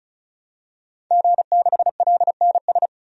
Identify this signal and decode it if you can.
Morse code, heard as a keyed tone, about 36 wpm, 700 Hz. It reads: G6LNS
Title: DXCC-100 Call Signs - 5 Characters - 35wpm